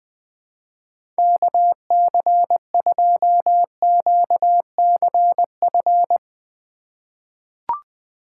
Morse code, heard as a keyed tone, about 20 wpm, 700 Hz. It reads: KC2QCF
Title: USA Call Signs - 6 Characters - 3x Character Spacing - 20wpm